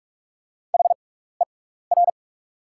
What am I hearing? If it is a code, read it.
Morse code, heard as a keyed tone, about 44 wpm, spaced as for 15 wpm, 690 Hz. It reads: HER
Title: 6x Character Spacing - Top 100 Words - 45wpm